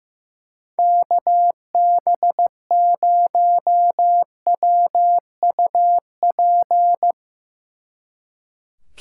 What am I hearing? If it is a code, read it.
Morse code, heard as a keyed tone, about 15 wpm, 700 Hz. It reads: KB0WUP
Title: Speed Racing - USA 2x3 Call Signs 15wpm